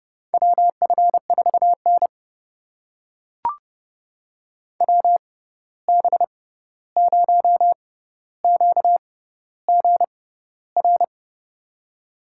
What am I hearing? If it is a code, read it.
Morse code, heard as a keyed tone, about 32 wpm, spaced as for 14 wpm, 700 Hz. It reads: WF4N WB0QGR
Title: USA Call Signs - 6x Character Spacing - 30wpm